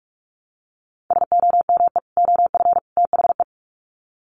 Morse code, heard as a keyed tone, about 44 wpm, 700 Hz. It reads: SOME OF THE
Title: Sets of 3 Words 45wpm